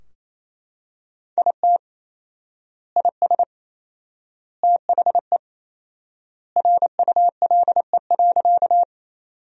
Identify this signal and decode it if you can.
Morse code, heard as a keyed tone, about 28 wpm, 700 Hz. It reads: IT IS THE RULE.
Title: 4x Word-Spacing - Sentences from Top 400 Words - 28wpm